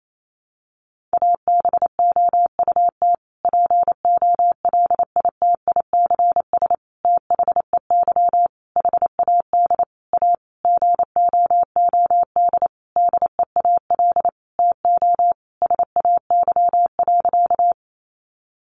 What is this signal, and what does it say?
Morse code, heard as a keyed tone, about 28 wpm, 700 Hz. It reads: ABOUT POLITICS THEY HAD A GOOD DEAL TO SAY.
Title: Sentences from Top 1000 Words 28wpm